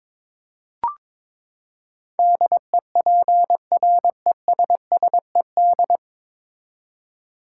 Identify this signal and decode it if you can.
Morse code, heard as a keyed tone, about 22 wpm, 700 Hz. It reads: DEPRESSED
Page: Morse Code Ninja practice 22wpm